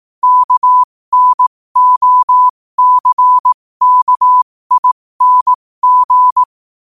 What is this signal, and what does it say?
Morse code, heard as a keyed tone, about 18 wpm, spaced as for 15 wpm, 1000 Hz. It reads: KNOCKING
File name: qotdfiles / 15wpm